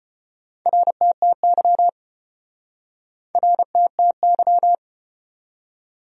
Morse code, heard as a keyed tone, about 34 wpm, 700 Hz. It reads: RTTY RTTY